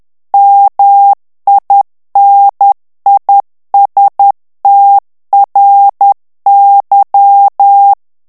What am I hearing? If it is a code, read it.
Morse code, heard as a keyed tone, about 10 wpm, 800 Hz. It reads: MINISTRY
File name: morse mp3 / ANGLAIS texte clair 2 vitesse 10 mots minute